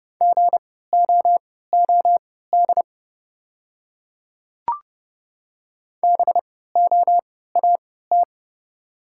Morse code, heard as a keyed tone, about 30 wpm, spaced as for 17 wpm, 700 Hz. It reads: GOOD BOAT
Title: 3x Character Spacing - Top 500 Words - 30wpm